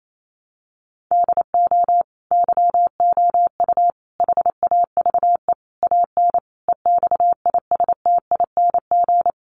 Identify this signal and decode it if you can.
Morse code, heard as a keyed tone, about 28 wpm, 700 Hz. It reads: DO YOU HAVE AN EXISTING
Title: Sentences from Top 1500 Words 28wpm